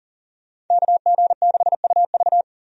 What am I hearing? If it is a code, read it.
Morse code, heard as a keyed tone, about 40 wpm, 700 Hz. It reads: KG6UV
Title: USA Call Signs - 5 Characters - 40wpm